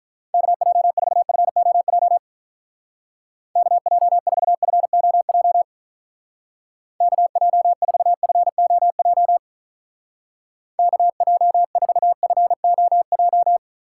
Morse code, heard as a keyed tone, about 42 wpm, 700 Hz. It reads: KJ4FOJ KJ4FOJ KJ4FOJ KJ4FOJ
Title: Speed Racing - USA 2x3 Call Signs 35wpm